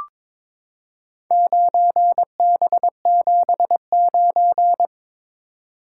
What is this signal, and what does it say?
Morse code, heard as a keyed tone, about 22 wpm, 700 Hz. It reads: 9B79
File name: practice / 22wpm / Four Letter-Number Combinations - 22wpm